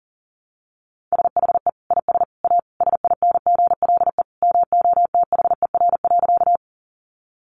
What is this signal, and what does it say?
Morse code, heard as a keyed tone, about 37 wpm, 700 Hz. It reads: SHE IS A SINGLE MOTHER.
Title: Sentences from Top 700 Words 40wpm